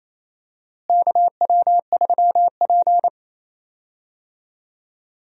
Morse code, heard as a keyed tone, about 28 wpm, 700 Hz. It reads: KW3P
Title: USA Call Signs - 4 Characters - Repeated 2x - 28wpm